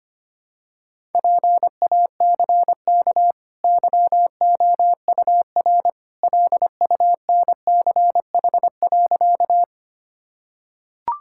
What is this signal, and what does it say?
Morse code, heard as a keyed tone, about 25 wpm, 700 Hz. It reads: PACK YOUR LUNCH.
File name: Easy Sentences with Hints - 25wpm